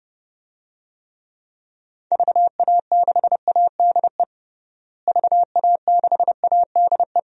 Morse code, heard as a keyed tone, about 30 wpm, 700 Hz. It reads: VA6ADE VA6ADE
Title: DXCC-100 Call Signs - 6 Characters - Repeated 2x - 30wpm